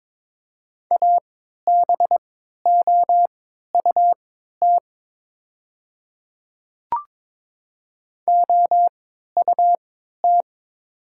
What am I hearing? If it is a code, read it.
Morse code, heard as a keyed tone, about 22 wpm, spaced as for 12 wpm, 700 Hz. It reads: ABOUT OUT
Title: 3x Character Spacing - Top 100 Words - 22wpm